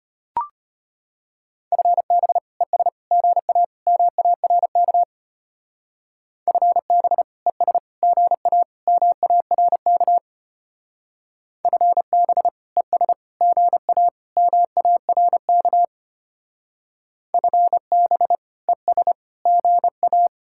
Morse code, heard as a keyed tone, about 31 wpm, 700 Hz. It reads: FB ES GA MARK FB ES GA MARK FB ES GA MARK FB ES GA
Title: Speed-Racing Common QSO Phrases Encore - 25wpm